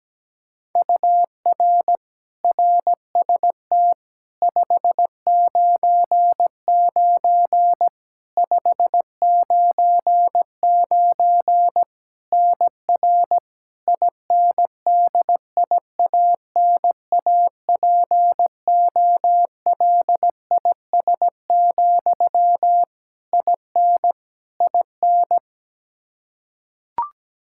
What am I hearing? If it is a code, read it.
Morse code, heard as a keyed tone, about 17 wpm, 700 Hz. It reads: UR RST 599 599 NR INDIANAPOLIS, IN IN
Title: Practice QSO - Exchange 1 - Beginning - 17wpm